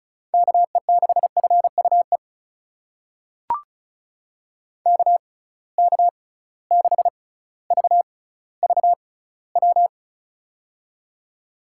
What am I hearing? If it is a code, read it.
Morse code, heard as a keyed tone, about 34 wpm, spaced as for 18 wpm, 700 Hz. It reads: KE6FUE KK6VVW
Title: USA Call Signs - 6 Characters - 6x Character Spacing - 35wpm